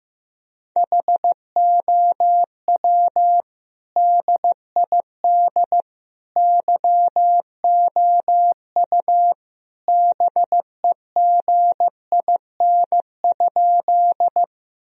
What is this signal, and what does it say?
Morse code, heard as a keyed tone, about 15 wpm, 700 Hz. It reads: HOW DID YOU BEGIN?
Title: Sentences from Top 1000 Words - Review Part 1 - 15wpm